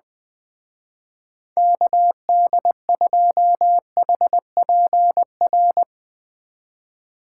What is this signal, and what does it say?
Morse code, heard as a keyed tone, about 20 wpm, 700 Hz. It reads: KD2HPR
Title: Speed Racing - USA 2x3 Call Signs 20wpm